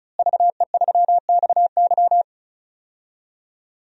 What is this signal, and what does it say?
Morse code, heard as a keyed tone, about 36 wpm, 700 Hz. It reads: VE3XY
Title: DXCC-100 Call Signs - 5 Characters - Repeated 3x - 35wpm